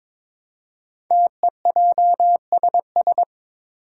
Morse code, heard as a keyed tone, about 22 wpm, 700 Hz. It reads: TEJSS
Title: Code Groups - 5 Letters - 22wpm